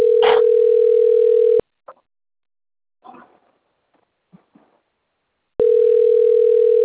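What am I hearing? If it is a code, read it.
Ringback tone (North America), 440 + 480 Hz.
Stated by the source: Outbound telephone call recording from Sao Carlos.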